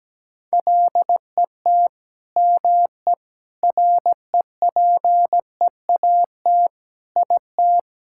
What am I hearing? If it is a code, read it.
Morse code, heard as a keyed tone, about 17 wpm, 700 Hz. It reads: LET ME REPEAT IT
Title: Sentences from Top 1600 Words 17wpm